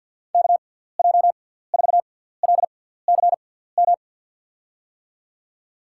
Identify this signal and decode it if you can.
Morse code, heard as a keyed tone, about 48 wpm, spaced as for 17 wpm, 700 Hz. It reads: KJ4LCM
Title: USA Call Signs - 6x Character Spacing - 50wpm